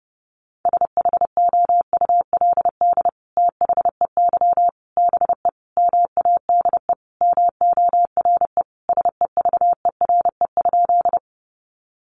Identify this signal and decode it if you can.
Morse code, heard as a keyed tone, about 30 wpm, 700 Hz. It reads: SHOULD THEY BE MADE MORE SEVERE?
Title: Sentences from Top 1400 Words 30wpm